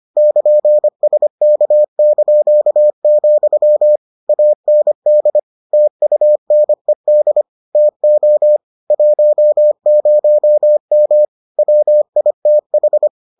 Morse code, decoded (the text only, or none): <KN>SK), AND TUNED TO 10M WITH